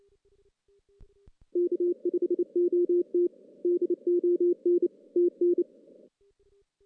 Morse code, heard as a keyed tone, about 29 wpm, 350 Hz. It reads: K5OT DON TN
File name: 2016 JAN CW NAQP